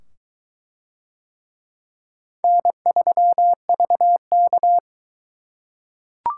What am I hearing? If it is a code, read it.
Morse code, heard as a keyed tone, about 22 wpm, 700 Hz. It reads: N3VK